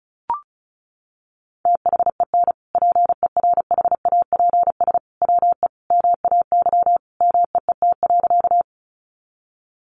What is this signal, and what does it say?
Morse code, heard as a keyed tone, about 34 wpm, 700 Hz. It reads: THEN PERHAPS WE MAY MEET.
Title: Sentences from Top 1100 Words 35wpm